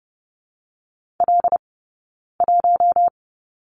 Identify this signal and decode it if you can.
Morse code, heard as a keyed tone, about 30 wpm, 700 Hz. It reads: L 1
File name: Lesson 16d - Letter c - Two Characters 30wpm